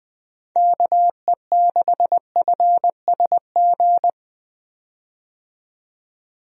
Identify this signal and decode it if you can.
Morse code, heard as a keyed tone, about 20 wpm, 700 Hz. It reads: KE6FSG
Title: USA Call Signs - 6 Characters - 20wpm